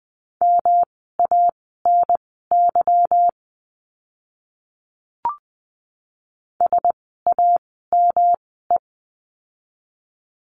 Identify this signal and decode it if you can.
Morse code, heard as a keyed tone, about 20 wpm, spaced as for 14 wpm, 700 Hz. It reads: MANY SAME
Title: 244 Lesson 39f - Letter z - Words 20wpm